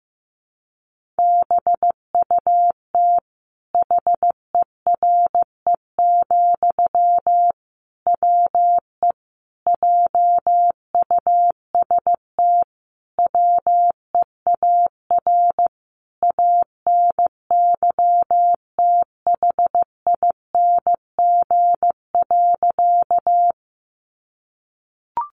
Morse code, heard as a keyed tone, about 15 wpm, 700 Hz. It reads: BUT HERE, WE JUST WEAR ANYTHING.
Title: Sentences from Top 1500 Words 15wpm